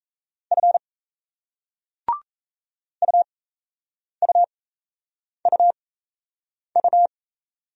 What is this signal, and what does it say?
Morse code, heard as a keyed tone, about 38 wpm, spaced as for 8 wpm, 700 Hz. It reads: F UUUU